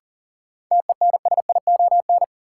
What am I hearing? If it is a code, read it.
Morse code, heard as a keyed tone, about 40 wpm, 700 Hz. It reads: TENSION
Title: Sentences from Top 1600 Words - New Word Drill - 40wpm